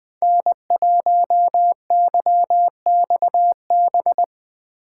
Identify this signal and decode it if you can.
Morse code, heard as a keyed tone, about 21 wpm, 700 Hz. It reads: N1YXB
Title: Speed Racing - USA 1x3 Call Signs 20wpm